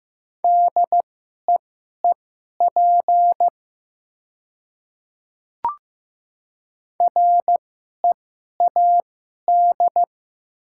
Morse code, heard as a keyed tone, about 15 wpm, spaced as for 11 wpm, 700 Hz. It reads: DEEP READ
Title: 2x Character Spacing - Top 500 Words - 15wpm